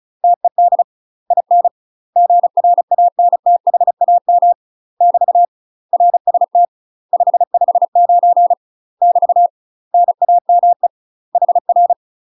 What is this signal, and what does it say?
Morse code, heard as a keyed tone, about 35 wpm, 700 Hz. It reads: TED IN GRANTHAM <BT> RST 559 <BT> NAME HR
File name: UK MASTER QSOs TRAINING / 35WPM / UK MASTER QSOs 0015